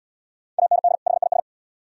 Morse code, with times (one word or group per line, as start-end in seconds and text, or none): 0.6-0.9 AND
1.1-1.4 HIS